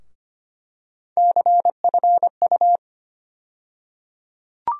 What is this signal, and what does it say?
Morse code, heard as a keyed tone, about 25 wpm, 700 Hz. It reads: CFU
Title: USA Call Sign Postfixes - 3 Letters 25wpm